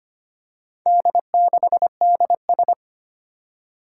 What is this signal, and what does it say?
Morse code, heard as a keyed tone, about 25 wpm, 700 Hz. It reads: D6DS
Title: Four Challenging Letter-Number Combinations 25wpm